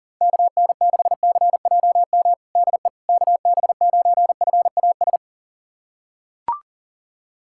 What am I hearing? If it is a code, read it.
Morse code, heard as a keyed tone, about 37 wpm, 700 Hz. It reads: KN6CJM DE KB9FAS
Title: USA Callsign DE Callsign 40wpm